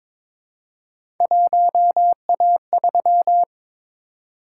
Morse code, heard as a keyed tone, about 22 wpm, 700 Hz. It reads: 1A3
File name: practice / 22wpm / Three Letter-Number Combinations - 22wpm